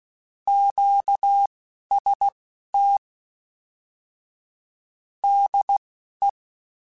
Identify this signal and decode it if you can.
Morse code, heard as a keyed tone, about 16 wpm, spaced as for 11 wpm, 800 Hz. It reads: QST DE
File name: Combinations / Combinations-mp3s / Combinations@10